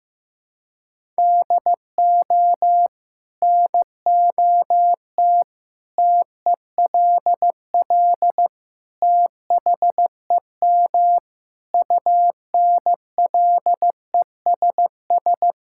Morse code, heard as a keyed tone, about 15 wpm, 700 Hz. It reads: DO NOT TELL THEM UNLESS
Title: Sentences from Top 1000 Words - Review Part 6 - 15wpm